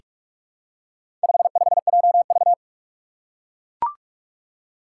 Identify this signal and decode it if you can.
Morse code, heard as a keyed tone, about 41 wpm, 700 Hz. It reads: 55JV
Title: Four Challenging Letter-Number Combinations 45wpm